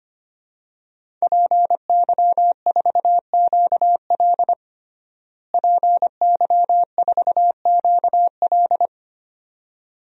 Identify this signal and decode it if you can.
Morse code, heard as a keyed tone, about 25 wpm, 700 Hz. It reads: PY4QL PY4QL
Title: DXCC-100 Call Signs - 5 Characters - Repeated 2x - 25wpm